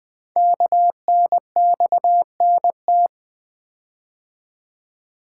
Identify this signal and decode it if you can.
Morse code, heard as a keyed tone, about 20 wpm, 700 Hz. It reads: KNXNT
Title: Code Groups - 5 Letters - 20wpm